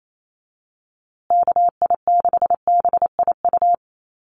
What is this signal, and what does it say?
Morse code, heard as a keyed tone, about 28 wpm, 690 Hz. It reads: KI6BIU
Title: USA Call Signs Repeated 3x 28wpm